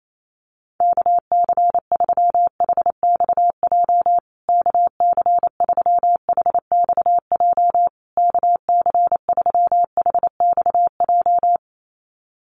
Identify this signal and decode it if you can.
Morse code, heard as a keyed tone, about 28 wpm, 700 Hz. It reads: KC3HXJ KC3HXJ KC3HXJ
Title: USA Call Signs Repeated 3x 28wpm